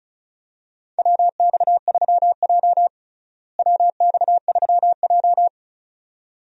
Morse code, heard as a keyed tone, about 36 wpm, 700 Hz. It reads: WX3J WX3J
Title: USA Call Signs - 4 Characters - Repeated 2x - 35wpm